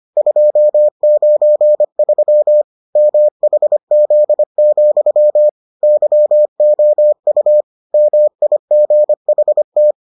Morse code, heard as a keyed tone, about 25 wpm, 600 Hz. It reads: 293 MHZ, YOU MIGHT